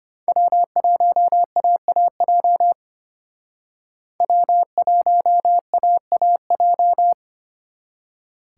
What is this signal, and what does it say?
Morse code, heard as a keyed tone, about 32 wpm, 700 Hz. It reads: W1AAJ W1AAJ